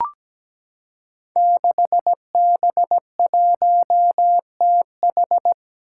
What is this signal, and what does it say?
Morse code, heard as a keyed tone, about 17 wpm, 700 Hz. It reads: 6B1TH
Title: Five Challenging Letter-Number Combinations 17wpm